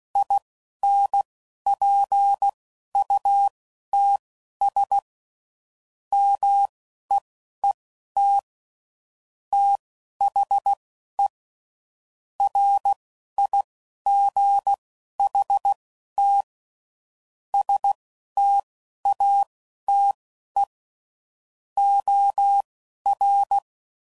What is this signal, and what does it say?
Morse code, heard as a keyed tone, about 16 wpm, spaced as for 11 wpm, 800 Hz. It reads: INPUTS MEET THE RIGHT STATE OR
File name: CW Code Groups @10WPM / Technical Subjects / OR Gate